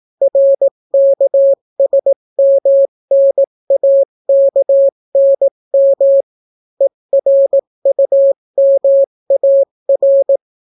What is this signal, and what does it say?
Morse code, heard as a keyed tone, about 18 wpm, 550 Hz. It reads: RKSMNAKNM ERUMAR